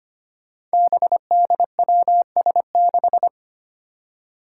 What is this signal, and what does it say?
Morse code, heard as a keyed tone, about 25 wpm, 700 Hz. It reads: BDWS6
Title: Five Challenging Letter-Number Combinations 25wpm